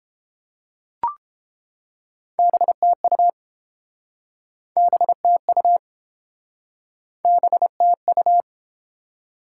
Morse code, heard as a keyed tone, about 30 wpm, 700 Hz. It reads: BTU BTU BTU